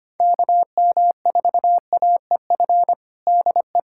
Morse code, heard as a keyed tone, about 25 wpm, 700 Hz. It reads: KM4AEF DE
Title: USA Callsign DE Callsign - 6 Characters - 25wpm